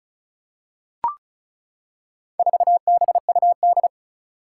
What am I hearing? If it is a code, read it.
Morse code, heard as a keyed tone, about 36 wpm, 700 Hz. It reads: 4BUD
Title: Four Challenging Letter-Number Combinations 35wpm